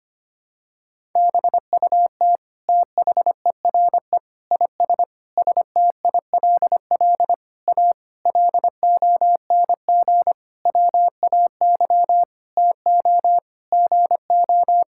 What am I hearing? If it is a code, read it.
Morse code, heard as a keyed tone, about 25 wpm, 700 Hz. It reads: BUT THERE IS STILL A LONG WAY TO GO
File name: practice / 25wpm / Sentences with 9-Word Phrases - 25wpm